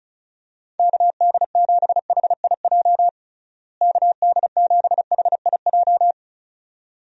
Morse code, heard as a keyed tone, about 34 wpm, 700 Hz. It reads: KD7HIJ KD7HIJ
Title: USA Call Signs - 6 Characters - Repeated 2x - 35wpm